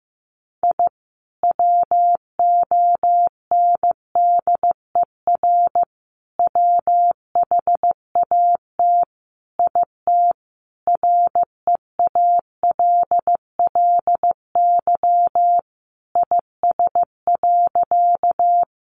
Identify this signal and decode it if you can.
Morse code, heard as a keyed tone, about 14 wpm, 700 Hz. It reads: I WONDER WHAT IT REALLY IS.
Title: Sentences from Top 600 Words 15wpm